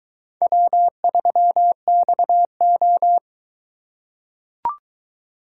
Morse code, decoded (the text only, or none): W3XO